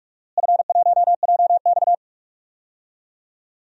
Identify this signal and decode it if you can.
Morse code, heard as a keyed tone, about 47 wpm, 700 Hz. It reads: F1JX
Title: DXCC-100 Call Signs - 4 Characters - Repeated 3x - 45wpm